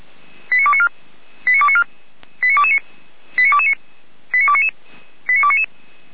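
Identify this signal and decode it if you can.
Five-tone selective call (ZVEI), heard as quick runs of five tones, tones of about 70 ms. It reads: 89294 89294 89229 89229 89299 89299
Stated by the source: ZVEI five-tone selective call received on SDR, logged by feuerwehr-orlamuende.org (Feuerwehr Orlamünde fire brigade).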